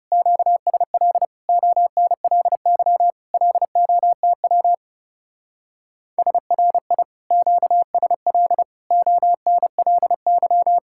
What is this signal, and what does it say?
Morse code, heard as a keyed tone, about 37 wpm, 700 Hz. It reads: QSL ONLY LOTW SRI QSL ONLY
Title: Speed-Racing Common QSO Phrases - 30wpm